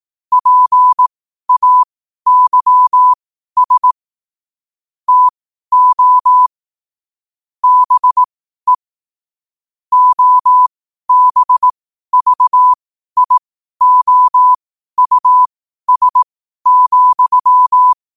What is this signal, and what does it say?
Morse code, heard as a keyed tone, about 18 wpm, spaced as for 12 wpm, 1000 Hz. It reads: PAYS TO BE OBVIOUS,
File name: qotdfiles / 12wpm